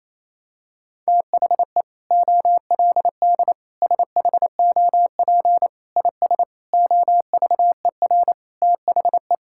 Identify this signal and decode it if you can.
Morse code, heard as a keyed tone, about 27 wpm, 700 Hz. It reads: THE OLD SHOP IS OVER THE
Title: Sentences from Top 1600 Words 28wpm